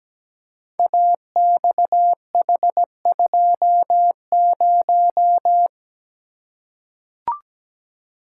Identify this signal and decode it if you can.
Morse code, heard as a keyed tone, about 17 wpm, 700 Hz. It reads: AXH20
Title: Code Groups - 5 Letter:Numbers - 17wpm